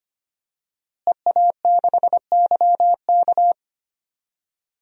Morse code, heard as a keyed tone, about 25 wpm, 700 Hz. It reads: EA6YK
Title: DXCC-100 Call Signs - 5 Characters - Repeated 3x - 25wpm